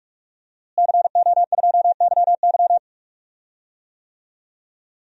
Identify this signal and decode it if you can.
Morse code, heard as a keyed tone, about 41 wpm, 700 Hz. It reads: CO2YY
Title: DXCC-100 Call Signs - 5 Characters - Repeated 3x - 45wpm